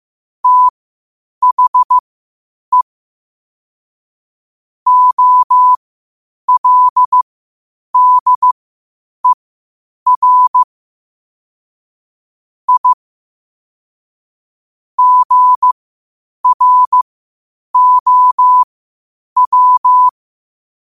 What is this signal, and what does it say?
Morse code, heard as a keyed tone, about 15 wpm, spaced as for 8 wpm, 1000 Hz. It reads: THE OLDER I GROW
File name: qotdfiles / 8wpm